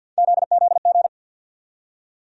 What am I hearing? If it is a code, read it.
Morse code, heard as a keyed tone, about 51 wpm, 700 Hz. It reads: ZZG
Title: USA Call Sign Postfixes - 3 Letters 50wpm